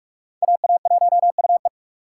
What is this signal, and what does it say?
Morse code, heard as a keyed tone, about 44 wpm, 700 Hz. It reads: AA1UE